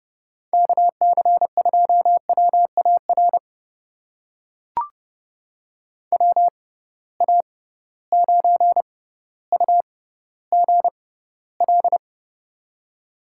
Morse code, heard as a keyed tone, about 31 wpm, spaced as for 15 wpm, 700 Hz. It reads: KC2WAR WA9UGL
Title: USA Call Signs - 6 Characters - 6x Character Spacing - 30wpm